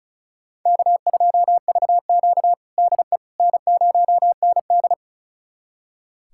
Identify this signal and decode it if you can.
Morse code, heard as a keyed tone, about 36 wpm, 700 Hz. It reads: K2VQ DE N0ND
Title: USA Callsign DE Callsign - 4 Characters - 35wpm